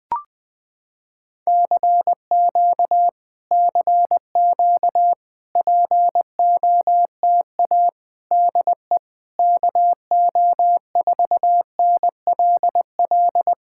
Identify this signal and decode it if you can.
Morse code, heard as a keyed tone, about 21 wpm, 700 Hz. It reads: CQ CQ POTA DE KO4NLL
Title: POTA QSO Practice - 20wpm